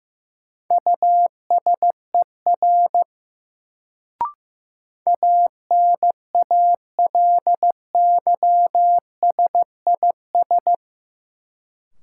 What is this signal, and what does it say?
Morse code, heard as a keyed tone, about 15 wpm, 700 Hz. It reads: USER ANALYSIS
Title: Sentences from Top 1000 Words - Review Part 5 - New Word Drill - 15wpm